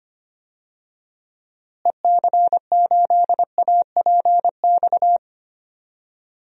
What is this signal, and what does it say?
Morse code, heard as a keyed tone, about 25 wpm, 700 Hz. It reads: EC8APX
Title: DXCC-100 Call Signs - 6 Characters - Repeated 3x - 25wpm